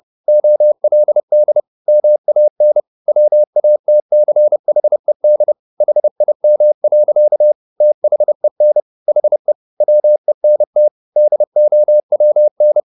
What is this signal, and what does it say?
Morse code, heard as a keyed tone, about 30 wpm, 600 Hz. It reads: OLD MAN WATCHED HIM. THEN HE WENT DOWN